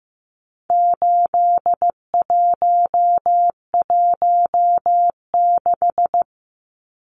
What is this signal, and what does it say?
Morse code, heard as a keyed tone, about 15 wpm, 700 Hz. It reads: 8116
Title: Numbers - 4-Digit 15wpm